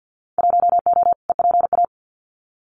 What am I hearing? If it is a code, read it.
Morse code, heard as a keyed tone, about 50 wpm, 700 Hz. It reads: 1O EPA
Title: Winter Field Day Exchanges 50wpm